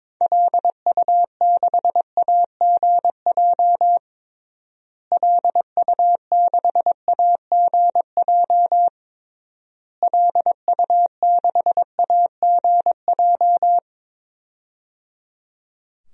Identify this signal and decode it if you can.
Morse code, heard as a keyed tone, about 22 wpm, 700 Hz. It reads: LU6AGJ LU6AGJ LU6AGJ
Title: DXCC-100 Call Signs - 6 Characters - Repeated 3x - 22wpm